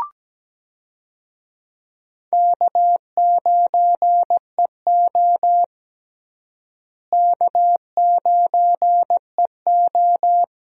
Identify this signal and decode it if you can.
Morse code, heard as a keyed tone, about 17 wpm, 700 Hz. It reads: K9EO K9EO
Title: USA Call Signs - 4 Characters - Repeated 3x - 17wpm